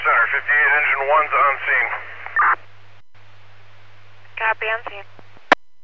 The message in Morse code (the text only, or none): none